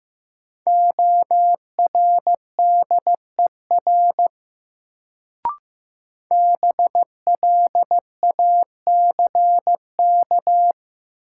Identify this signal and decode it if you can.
Morse code, heard as a keyed tone, about 15 wpm, 700 Hz. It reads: ORDER BLACK
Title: Sentences from Top 400 Words - New Word Drill - 15wpm